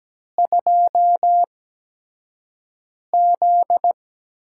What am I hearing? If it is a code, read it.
Morse code, heard as a keyed tone, about 17 wpm, spaced as for 5 wpm, 700 Hz. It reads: 2Z
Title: USA Call Signs - 4 Characters - 8x Character Spacing - 17wpm